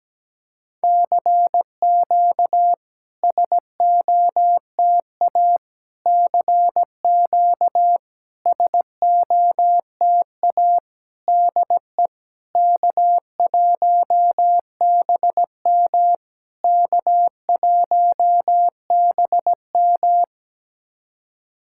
Morse code, heard as a keyed tone, about 17 wpm, 690 Hz. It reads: CQ SOTA CQ SOTA DE K1BM K1BM
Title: SOTA QSO Practice 17wpm